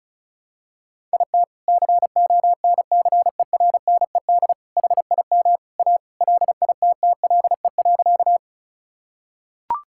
Morse code, heard as a keyed tone, about 36 wpm, 700 Hz. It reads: IT CONCERNED HIM A LITTLE.
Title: Sentences from Top 1000 Words - Review Part 6 - 35wpm